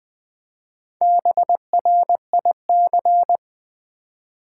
Morse code, heard as a keyed tone, about 20 wpm, 700 Hz. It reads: BRIC
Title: Build A Word - 1 Letter At A Time - 3 to 5-Letter Words - 20wpm